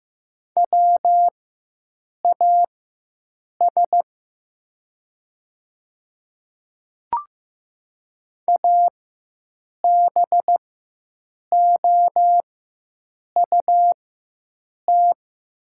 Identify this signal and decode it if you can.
Morse code, heard as a keyed tone, about 15 wpm, spaced as for 7 wpm, 700 Hz. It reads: WAS ABOUT
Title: 4x Character Spacing - Top 100 Words - 15wpm